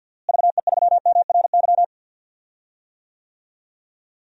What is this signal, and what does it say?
Morse code, heard as a keyed tone, about 52 wpm, 700 Hz. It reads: VE3MRY
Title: DXCC-100 Call Signs - 6 Characters - Repeated 3x - 50wpm